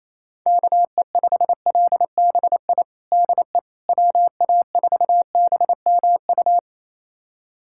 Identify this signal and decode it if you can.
Morse code, heard as a keyed tone, about 28 wpm, 700 Hz. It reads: KE5LBI DE WA4BMU
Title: USA Callsign DE Callsign - 6 Characters - 28wpm